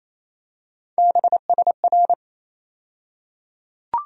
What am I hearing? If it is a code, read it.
Morse code, heard as a keyed tone, about 28 wpm, 700 Hz. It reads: BSR
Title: USA Call Sign Postfixes - 3 Letters 28wpm